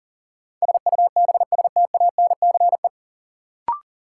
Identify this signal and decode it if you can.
Morse code, heard as a keyed tone, about 40 wpm, 700 Hz. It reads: SUBSTANCE
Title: Sentences from Top 1500 Words - New Word Drill - 40wpm